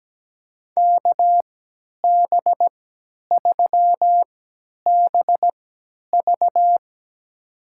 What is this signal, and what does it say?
Morse code, heard as a keyed tone, about 17 wpm, spaced as for 10 wpm, 700 Hz. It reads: KB3BV
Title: USA Call Signs - 5 Characters - 3x Character Spacing - 17wpm